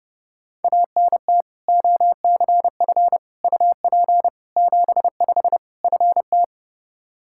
Morse code, heard as a keyed tone, about 32 wpm, 700 Hz. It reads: ANT OCF UP 75 FT
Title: Practice QSO - Exchange 2 - Info - 30wpm